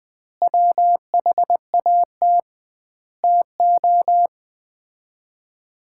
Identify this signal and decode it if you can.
Morse code, heard as a keyed tone, about 20 wpm, 700 Hz. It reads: WHAT TO
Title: 2x Word-Spacing - Sets of 5 Words - 20wpm